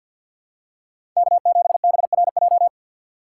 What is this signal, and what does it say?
Morse code, heard as a keyed tone, about 49 wpm, 700 Hz. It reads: K7BRJ
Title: USA Call Signs - 2x Character Spacing - 50wpm